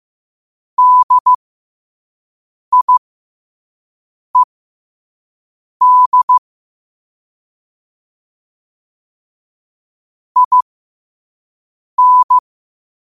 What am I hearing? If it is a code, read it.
Morse code, heard as a keyed tone, about 15 wpm, spaced as for 5 wpm, 1000 Hz. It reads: DIED IN
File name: qotdfiles / 5wpm